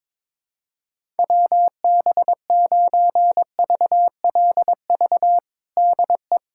Morse code, heard as a keyed tone, about 22 wpm, 700 Hz. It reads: WB9VLV DE
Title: USA Callsign DE Callsign 22wpm